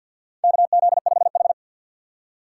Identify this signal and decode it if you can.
Morse code, heard as a keyed tone, about 52 wpm, 700 Hz. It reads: KZ5H
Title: USA Call Signs - 4 Characters - 2x Character Spacing - 50wpm